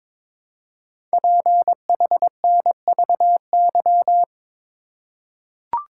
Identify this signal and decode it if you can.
Morse code, heard as a keyed tone, about 22 wpm, 700 Hz. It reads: PHNVY
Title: Code Groups - 5 Letters - 22wpm